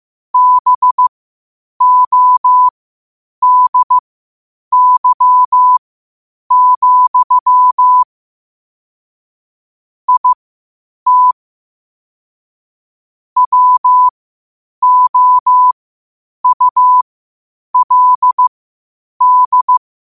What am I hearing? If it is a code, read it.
Morse code, heard as a keyed tone, about 15 wpm, spaced as for 8 wpm, 1000 Hz. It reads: BODY, IT WOULD
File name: qotdfiles / 8wpm